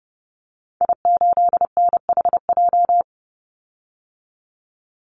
Morse code, heard as a keyed tone, about 30 wpm, 700 Hz. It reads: I8NHJ
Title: Call Signs - 30wpm